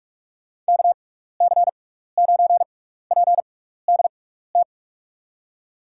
Morse code, heard as a keyed tone, about 44 wpm, spaced as for 15 wpm, 700 Hz. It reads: KC9PDT
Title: USA Call Signs - 6 Characters - 6x Character Spacing - 45wpm